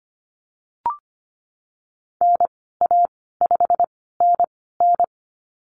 Morse code, heard as a keyed tone, about 25 wpm, spaced as for 16 wpm, 700 Hz. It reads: NA5NN
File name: Lesson 15g - Number 5 - Call Signs 25wpm